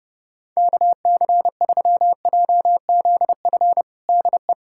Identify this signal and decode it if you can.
Morse code, heard as a keyed tone, about 30 wpm, 700 Hz. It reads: KC3JZF DE
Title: USA Callsign DE Callsign 30wpm